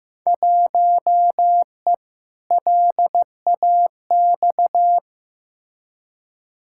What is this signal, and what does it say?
Morse code, heard as a keyed tone, about 15 wpm, 700 Hz. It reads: 1E LAX
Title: Field Day Exchanges - 15wpm